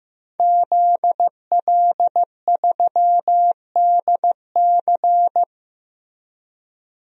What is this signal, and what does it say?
Morse code, heard as a keyed tone, about 15 wpm, 700 Hz. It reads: ZL3DC
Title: DXCC-100 Call Signs - 5 Characters - Repeated 3x - 15wpm